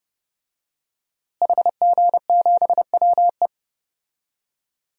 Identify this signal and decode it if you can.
Morse code, heard as a keyed tone, about 30 wpm, 700 Hz. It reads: HG7WE
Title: DXCC-100 Call Signs - 5 Characters - Repeated 2x - 30wpm